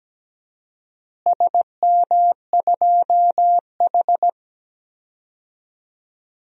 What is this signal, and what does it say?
Morse code, heard as a keyed tone, about 17 wpm, 700 Hz. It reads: SM2H
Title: DXCC-100 Call Signs - 4 Characters - Repeated 3x - 17wpm